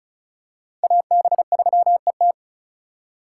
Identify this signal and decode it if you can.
Morse code, heard as a keyed tone, about 34 wpm, 700 Hz. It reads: AB3ET
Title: USA Call Signs - 5 Characters - 4x Character Spacing - 35wpm